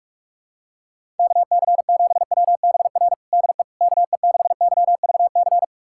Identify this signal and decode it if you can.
Morse code, heard as a keyed tone, about 41 wpm, 700 Hz. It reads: KC7WBR DE KE6YVC
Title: USA Callsign DE Callsign - 6 Characters - 45wpm